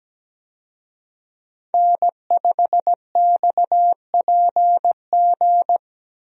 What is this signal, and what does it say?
Morse code, heard as a keyed tone, about 17 wpm, 700 Hz. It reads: N5XPG
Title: USA Call Signs - 5 Characters - Repeated 3x - 17wpm